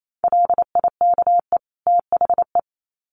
Morse code, heard as a keyed tone, about 28 wpm, 700 Hz. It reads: LIKE THE
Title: Sets of 2 Words 28wpm